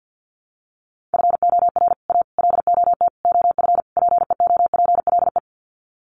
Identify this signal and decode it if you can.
Morse code, heard as a keyed tone, about 50 wpm, 700 Hz. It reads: FOR A LOT OF PEOPLE
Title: Sets of 5 Words Encore 50wpm